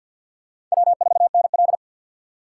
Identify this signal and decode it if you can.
Morse code, heard as a keyed tone, about 49 wpm, 700 Hz. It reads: W4NL
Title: USA Call Signs - 4 Characters - 2x Character Spacing - 50wpm